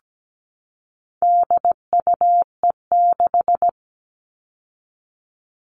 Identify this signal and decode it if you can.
Morse code, heard as a keyed tone, about 17 wpm, 700 Hz. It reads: DUE6
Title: All Cut Numbers - 4-Digit 17wpm